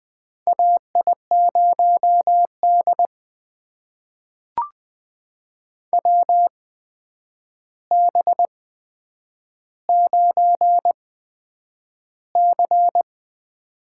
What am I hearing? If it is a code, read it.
Morse code, heard as a keyed tone, about 20 wpm, spaced as for 9 wpm, 700 Hz. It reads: AI0D WB9C
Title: USA Call Signs - 4 Characters - 8x Character Spacing - 20wpm